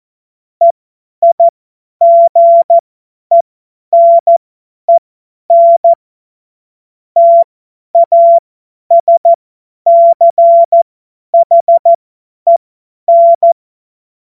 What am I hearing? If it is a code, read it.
Morse code, heard as a keyed tone, about 14 wpm, spaced as for 10 wpm, 680 Hz. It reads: EIGENEN TASCHEN
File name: Morsekurs / Witze / Witz 22. August 2023 50 F70